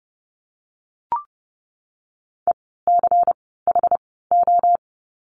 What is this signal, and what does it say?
Morse code, heard as a keyed tone, about 30 wpm, 700 Hz. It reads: E C H O
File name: Lesson 16f - Letter c - Words 30wpm